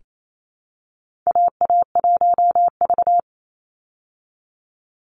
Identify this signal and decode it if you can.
Morse code, heard as a keyed tone, about 28 wpm, 700 Hz. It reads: AA1V
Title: Call Signs Repeated 2x - 28wpm